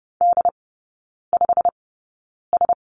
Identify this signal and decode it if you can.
Morse code, heard as a keyed tone, about 30 wpm, spaced as for 9 wpm, 700 Hz. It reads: D5S
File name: Lesson 18e - Letter m - Three Characters 30wpm